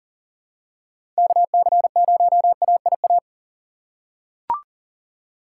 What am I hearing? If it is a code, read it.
Morse code, heard as a keyed tone, about 40 wpm, 700 Hz. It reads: KC0AIA
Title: USA Call Signs - 6x Character Spacing - 40wpm